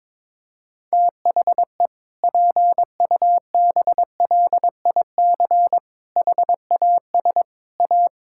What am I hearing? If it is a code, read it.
Morse code, heard as a keyed tone, about 22 wpm, 700 Hz. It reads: THE PUBLIC HAS A
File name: Sentences with 7-Word Phrases - 22wpm